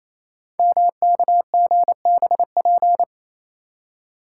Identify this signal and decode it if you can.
Morse code, heard as a keyed tone, about 28 wpm, 700 Hz. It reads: MKGBP
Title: Code Groups - 5 Letters - 28wpm